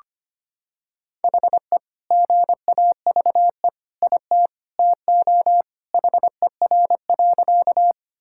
Morse code, heard as a keyed tone, about 25 wpm, 700 Hz. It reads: HE GAVE IT TO HER.
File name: practice / 25wpm / Sentences with 2-Word Phrases - 25wpm